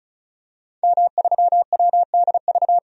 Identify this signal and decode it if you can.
Morse code, heard as a keyed tone, about 36 wpm, 700 Hz. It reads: M3WDV
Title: DXCC-100 Call Signs - 5 Characters - 35wpm